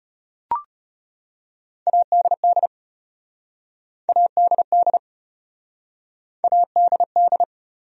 Morse code, heard as a keyed tone, about 35 wpm, 700 Hz. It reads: ADD ADD ADD